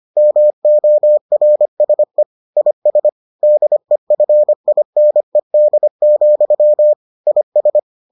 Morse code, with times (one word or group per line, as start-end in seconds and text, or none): 0.2-2.2 MORSE
2.6-3.1 IS
3.4-6.9 DEFINED,
7.3-7.8 IS